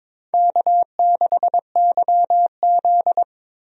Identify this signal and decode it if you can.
Morse code, heard as a keyed tone, about 21 wpm, 700 Hz. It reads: K6YZ
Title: USA Call Signs - 4 Characters - Repeated 3x - 22wpm